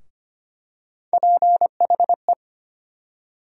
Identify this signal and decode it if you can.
Morse code, heard as a keyed tone, about 25 wpm, 700 Hz. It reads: PHE
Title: USA Call Sign Postfixes - 3 Letters 25wpm